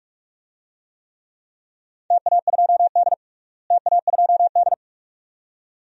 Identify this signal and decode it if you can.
Morse code, heard as a keyed tone, about 46 wpm, 700 Hz. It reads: TA2D TA2D
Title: DXCC-100 Call Signs - 4 Characters - Repeated 2x - 45wpm